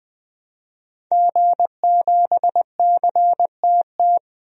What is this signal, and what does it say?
Morse code, heard as a keyed tone, about 20 wpm, 700 Hz. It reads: G7CTT
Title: DXCC-100 Call Signs - 5 Characters - Repeated 3x - 20wpm